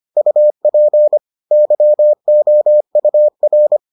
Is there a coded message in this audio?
Yes: Morse code from 0.2 to 3.8 s, heard as a keyed tone, about 25 wpm, 600 Hz. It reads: UP YOUR